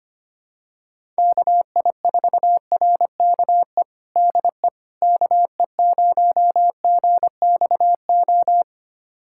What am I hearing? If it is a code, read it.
Morse code, heard as a keyed tone, about 25 wpm, 700 Hz. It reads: KI4RKE DE KE0GXO
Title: USA Callsign DE Callsign - 6 Characters - 25wpm